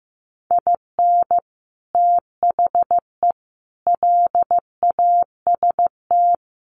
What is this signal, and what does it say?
Morse code, heard as a keyed tone, about 15 wpm, 700 Hz. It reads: IN THE LAST
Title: Sets of 5 Words 15wpm